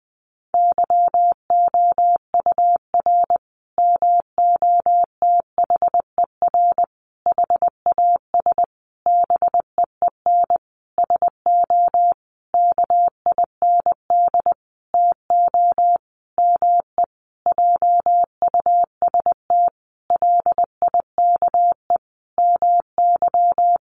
Morse code, heard as a keyed tone, about 20 wpm, 700 Hz. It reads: YOUR MOTHER HAS BEEN SO KIND TO ME JUST LIKE MY
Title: Sentences from Top 1200 Words 20wpm